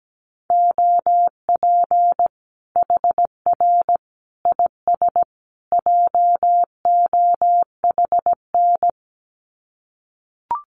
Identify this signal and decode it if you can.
Morse code, heard as a keyed tone, about 17 wpm, 700 Hz. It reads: OP HR IS JOHN
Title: Common QSO Phrases 17wpm